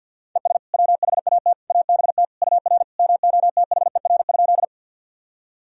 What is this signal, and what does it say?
Morse code, heard as a keyed tone, about 49 wpm, 700 Hz. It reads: ES WHAT ABT UR MOTHER?
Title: Sentences from Top 200 Words and Abbreviations 50wpm